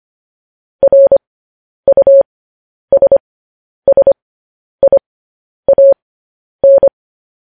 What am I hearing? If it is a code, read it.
Morse code, heard as a keyed tone, about 25 wpm, spaced as for 10 wpm, 550 Hz. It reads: RUSSIAN
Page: News Headlines in Morse Code at 10 WPM